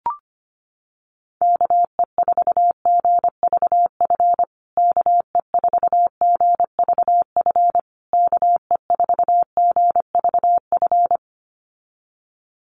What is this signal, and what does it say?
Morse code, heard as a keyed tone, about 25 wpm, 700 Hz. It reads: KE4GVF KE4GVF KE4GVF
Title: USA Call Signs Repeated 3x 25wpm